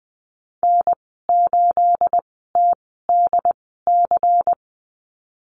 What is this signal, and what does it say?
Morse code, heard as a keyed tone, about 20 wpm, spaced as for 14 wpm, 700 Hz. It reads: N8TDC
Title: 245 Lesson 39g - Letter z - Call Signs 20wpm